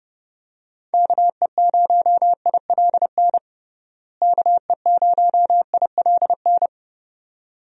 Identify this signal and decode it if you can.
Morse code, heard as a keyed tone, about 30 wpm, 700 Hz. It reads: KE0ILN KE0ILN
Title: USA Call Signs - 6 Characters - Repeated 2x - 30wpm